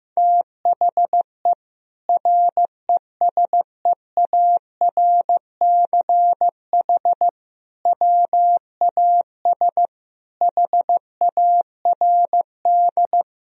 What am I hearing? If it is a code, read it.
Morse code, heard as a keyed tone, about 15 wpm, 700 Hz. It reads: THE RESEARCH WAS HARD
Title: Sentences from Top 1000 Words - Review Part 4 - 15wpm